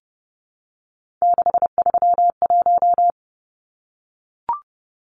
Morse code, heard as a keyed tone, about 30 wpm, 700 Hz. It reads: 631
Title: Numbers - 3-Digit 30wpm